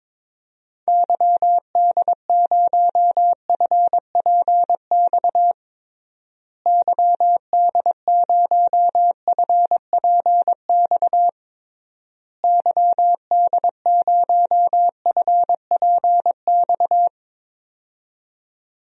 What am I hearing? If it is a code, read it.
Morse code, heard as a keyed tone, about 22 wpm, 700 Hz. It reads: YD0FPX YD0FPX YD0FPX
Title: DXCC-100 Call Signs - 6 Characters - Repeated 3x - 22wpm